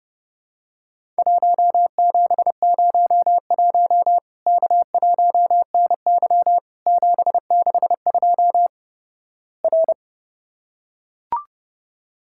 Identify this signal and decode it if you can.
Morse code, heard as a keyed tone, about 29 wpm, 700 Hz. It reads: 1701 K1NY 762 R
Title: Worked All Europe DX Contest - QTC Exchange - 30wpm